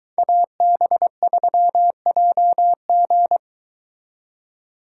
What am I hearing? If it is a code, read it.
Morse code, heard as a keyed tone, about 22 wpm, 700 Hz. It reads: AB3JG